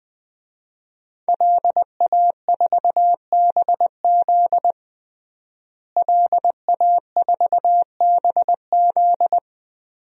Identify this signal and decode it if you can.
Morse code, heard as a keyed tone, about 20 wpm, 700 Hz. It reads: LA4BZ LA4BZ
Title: DXCC-100 Call Signs - 5 Characters - Repeated 2x - 20wpm